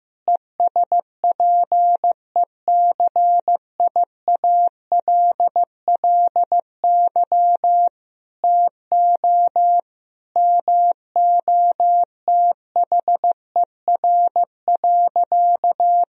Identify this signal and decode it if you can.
Morse code, heard as a keyed tone, about 15 wpm, 700 Hz. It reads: ESPECIALLY TO MOTHER.
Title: Sentences from Top 1000 Words - Review Part 4 - 15wpm